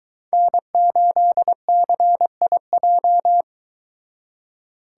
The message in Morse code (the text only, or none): N8CIJ